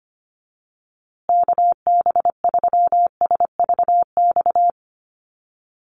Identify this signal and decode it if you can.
Morse code, heard as a keyed tone, about 25 wpm, 700 Hz. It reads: KB3SVX
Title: USA Call Signs Repeated 2x 25wpm